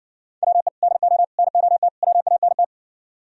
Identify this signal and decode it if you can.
Morse code, heard as a keyed tone, about 62 wpm, 700 Hz. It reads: WE DO NOT WANT